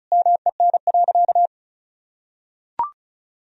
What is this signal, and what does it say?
Morse code, heard as a keyed tone, about 36 wpm, 700 Hz. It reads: MEN.
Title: Sentences from Top 1000 Words - Review Part 1 - 35wpm